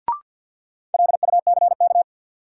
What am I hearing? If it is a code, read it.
Morse code, heard as a keyed tone, about 53 wpm, 700 Hz. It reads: LUCK
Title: Sentences from Top 1500 Words - New Word Drill - 50wpm